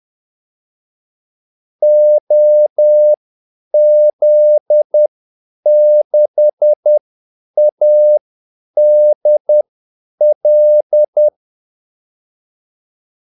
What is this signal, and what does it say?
Morse code, heard as a keyed tone, about 10 wpm, spaced as for 8 wpm, 600 Hz. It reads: OZ6ADL
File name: Callsigns-10-8wpm